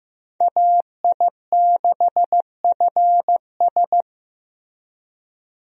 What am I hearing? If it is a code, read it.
Morse code, heard as a keyed tone, about 15 wpm, 700 Hz. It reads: AI6FS
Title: USA Call Signs - 5 Characters - Repeated 3x - 15wpm